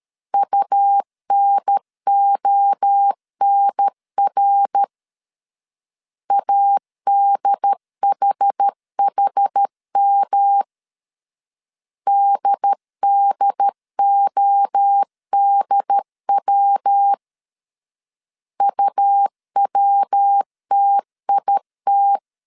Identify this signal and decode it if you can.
Morse code, heard as a keyed tone, about 13 wpm, 800 Hz. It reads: UNONR ADHHM DDODW UWTIT
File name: CW / DET / 1 3 1500C S C 12wpm 2750